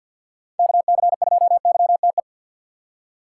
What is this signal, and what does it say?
Morse code, heard as a keyed tone, about 49 wpm, 700 Hz. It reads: KC2YTE
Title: USA Call Signs - 6 Characters - 6x Character Spacing - 50wpm